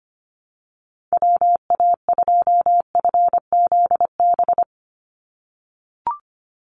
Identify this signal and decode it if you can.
Morse code, heard as a keyed tone, about 25 wpm, 700 Hz. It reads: WA2FZB
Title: Call Signs - 25wpm